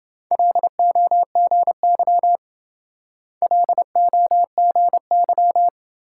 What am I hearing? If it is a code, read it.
Morse code, heard as a keyed tone, about 26 wpm, 700 Hz. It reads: LOGY LOGY